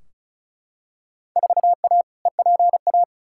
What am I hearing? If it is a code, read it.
Morse code, heard as a keyed tone, about 34 wpm, 700 Hz. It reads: 4A EPA
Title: Field Day Exchanges - 35wpm